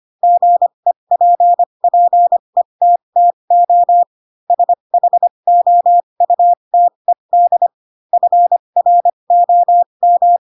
Morse code, heard as a keyed tone, about 25 wpm, spaced as for 22 wpm, 700 Hz. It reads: GEPPETTO SHOUTED FROM
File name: Pinocchio / Pinocchio ch7 18 25 ik1hge